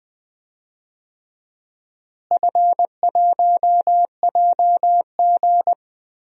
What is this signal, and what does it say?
Morse code, heard as a keyed tone, about 20 wpm, 700 Hz. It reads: F1JG
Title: DXCC-100 Call Signs - 4 Characters - Repeated 3x - 20wpm